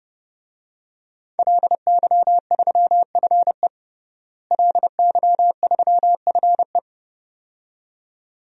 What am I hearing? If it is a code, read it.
Morse code, heard as a keyed tone, about 32 wpm, 700 Hz. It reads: LY3FE LY3FE
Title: DXCC-100 Call Signs - 5 Characters - Repeated 2x - 30wpm